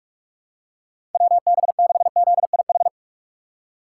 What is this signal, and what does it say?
Morse code, heard as a keyed tone, about 44 wpm, 700 Hz. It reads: WB6ZIH
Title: USA Call Signs - 6 Characters - 4x Character Spacing - 45wpm